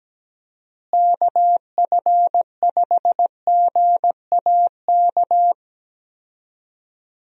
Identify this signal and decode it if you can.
Morse code, heard as a keyed tone, about 17 wpm, 700 Hz. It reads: KF5GAK
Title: USA Call Signs - 6 Characters - Repeated 3x - 17wpm